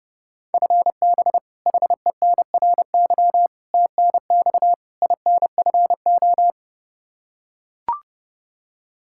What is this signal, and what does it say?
Morse code, heard as a keyed tone, about 30 wpm, 700 Hz. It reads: FB HENRY TNX INFO
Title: Practice QSO - Exchange 3 - More Info - 30wpm